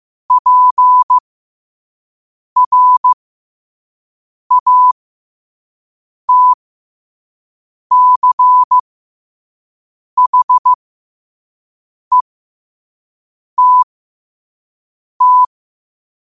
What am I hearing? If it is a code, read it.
Morse code, heard as a keyed tone, about 15 wpm, spaced as for 5 wpm, 1000 Hz. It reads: PRATCHETT
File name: qotdfiles / 5wpm